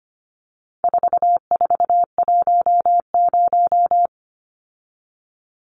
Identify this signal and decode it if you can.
Morse code, heard as a keyed tone, about 25 wpm, 700 Hz. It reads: 4410
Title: Numbers - 4-Digit 25wpm